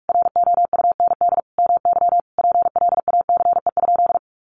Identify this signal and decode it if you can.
Morse code, heard as a keyed tone, about 44 wpm, 700 Hz. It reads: ROUND MY PLACE?
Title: Sentences from Top 200 Words 45wpm